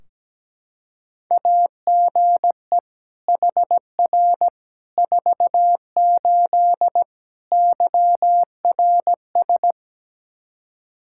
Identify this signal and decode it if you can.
Morse code, heard as a keyed tone, about 17 wpm, 700 Hz. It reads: AGE HR 48 YRS
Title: K1USN SST QSO Practice 17wpm